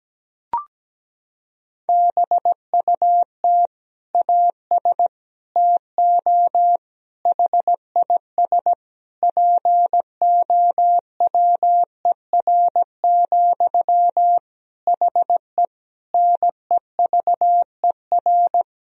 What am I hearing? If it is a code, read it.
Morse code, heard as a keyed tone, about 17 wpm, 700 Hz. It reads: BUT AS TO HIS POWER, HE NEVER
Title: Sentences from Top 1000 Words - Review Part 2 - 17wpm